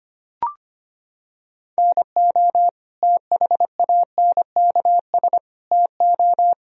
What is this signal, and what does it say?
Morse code, heard as a keyed tone, about 25 wpm, 700 Hz. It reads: NO THANKS TO
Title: Sentences from Top 1000 Words - Review Part 3 - 25wpm